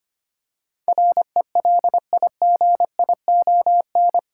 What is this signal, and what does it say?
Morse code, heard as a keyed tone, about 25 wpm, 700 Hz. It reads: RELIGION
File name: 8 Letter Words - First 3 Letters Given - 25wpm